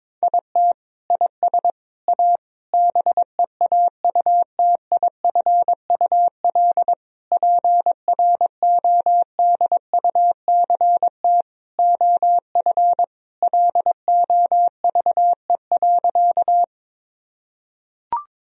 Morse code, heard as a keyed tone, about 22 wpm, 700 Hz. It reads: IT IS A BEAUTIFUL PRODUCT OF LOVE.
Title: Sentences from Top 1000 Words - Review Part 2 - 22wpm